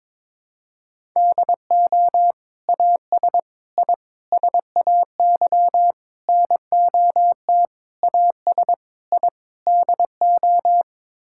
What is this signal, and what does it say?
Morse code, heard as a keyed tone, about 22 wpm, 700 Hz. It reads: DO AS I SAY NOT AS I DO
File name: practice / 22wpm / Sets of 8 Words - 22wpm